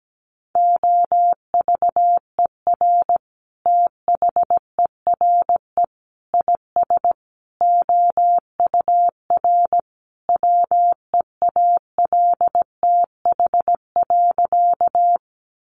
Morse code, heard as a keyed tone, about 17 wpm, 700 Hz. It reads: OVER THERE IS OUR WEALTH.
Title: Sentences from Top 1200 Words 17wpm